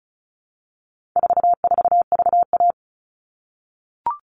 Morse code, heard as a keyed tone, about 34 wpm, 700 Hz. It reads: 44VA
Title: All Cut Numbers - 4-Digit 35wpm